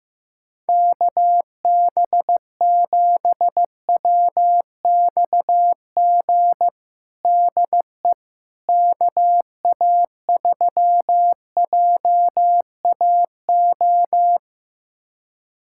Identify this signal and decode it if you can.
Morse code, heard as a keyed tone, about 15 wpm, 700 Hz. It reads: KB7WXG DE KA3JAO
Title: USA Callsign DE Callsign 15wpm